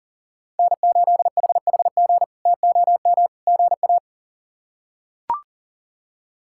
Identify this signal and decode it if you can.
Morse code, heard as a keyed tone, about 40 wpm, 700 Hz. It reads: N8HHG TOM GA
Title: K1USN SST QSO Practice - 40wpm